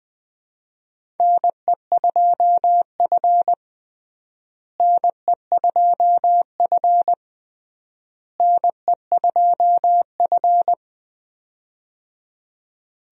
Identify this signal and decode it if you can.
Morse code, heard as a keyed tone, about 20 wpm, 700 Hz. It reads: NE2F NE2F NE2F
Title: USA Call Signs - 4 Characters - Repeated 3x - 20wpm